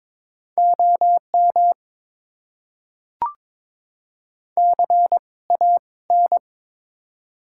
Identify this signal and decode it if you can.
Morse code, heard as a keyed tone, about 22 wpm, spaced as for 18 wpm, 700 Hz. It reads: OM CAN
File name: Top 90 QSO Elements - 2x Character Spacing - 22wpm